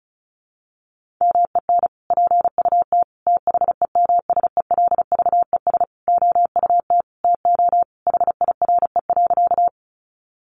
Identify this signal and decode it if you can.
Morse code, heard as a keyed tone, about 34 wpm, 700 Hz. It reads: MEN PUT THEMSELVES OUT TO HIRE.
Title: Sentences from Top 1400 Words 35wpm